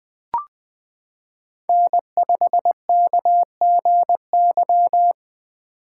Morse code, heard as a keyed tone, about 20 wpm, 700 Hz. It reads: N5KGY
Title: 2023 POTA Hunter Call Signs - 20wpm